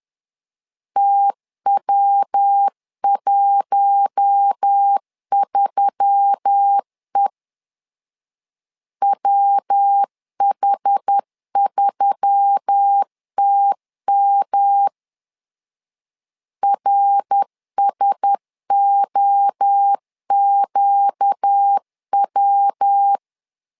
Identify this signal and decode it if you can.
Morse code, heard as a keyed tone, about 10 wpm, 800 Hz. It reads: TW13E WH3TM RSOQW
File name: LEZIONI / 4 Q1S3 10wpm 2427